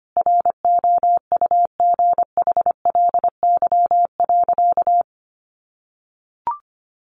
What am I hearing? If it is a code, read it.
Morse code, heard as a keyed tone, about 25 wpm, 700 Hz. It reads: ROUGHLY.
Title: Sentences from Top 1500 Words 25wpm